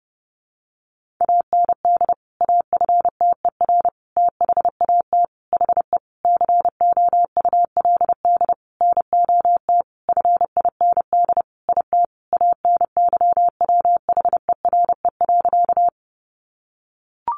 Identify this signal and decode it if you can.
Morse code, heard as a keyed tone, about 30 wpm, 700 Hz. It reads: AND AFTER THAT HE COULD NOT FIND IT ANYWHERE.
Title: Sentences from Top 1400 Words 30wpm